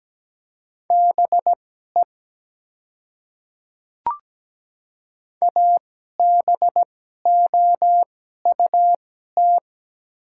Morse code, heard as a keyed tone, about 17 wpm, spaced as for 12 wpm, 700 Hz. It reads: BE ABOUT
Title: 2x Character Spacing - Top 300 Words - 17wpm